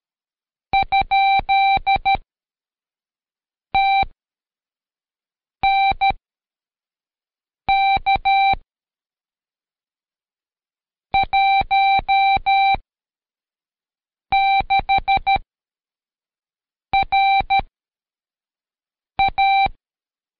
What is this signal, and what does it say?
Morse code, heard as a keyed tone, about 13 wpm, 780 Hz. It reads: ? T N K 1 6 R A
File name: Code Quick / AudioDownld2 / Seg104-Day12-FiinalExam 1